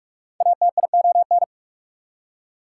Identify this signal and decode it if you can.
Morse code, heard as a keyed tone, about 44 wpm, 690 Hz. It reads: ATION
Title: Prefixes and Suffixes 45wpm